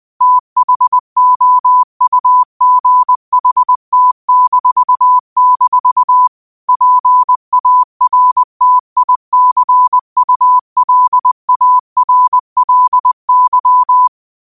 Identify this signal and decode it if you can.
Morse code, heard as a keyed tone, about 20 wpm, 1000 Hz. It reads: THOUGHT-- PARTICULARLY
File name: qotdfiles / 20wpm